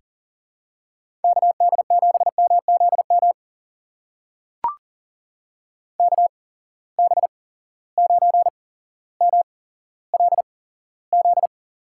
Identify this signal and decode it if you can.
Morse code, heard as a keyed tone, about 40 wpm, spaced as for 17 wpm, 700 Hz. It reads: KD7MZM KB9MLZ
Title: USA Call Signs - 6 Characters - 8x Character Spacing - 40wpm